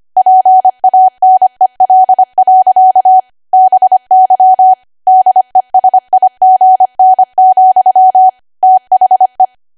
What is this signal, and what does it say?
Morse code, heard as a keyed tone, about 25 wpm, 750 Hz. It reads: PANEL. BY DESIGN, THE
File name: Archive / 25 WPM / 25WPM